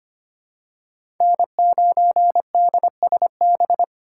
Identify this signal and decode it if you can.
Morse code, heard as a keyed tone, about 25 wpm, 700 Hz. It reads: N9DSB
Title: USA Call Signs - 5 Characters - 4x Character Spacing - 25wpm